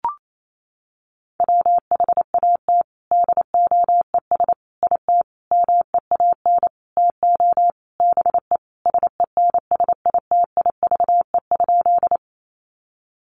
Morse code, heard as a keyed tone, about 28 wpm, 700 Hz. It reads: WHAT DOES IT MEAN TO BE SENSITIVE?
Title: Sentences from Top 1400 Words 28wpm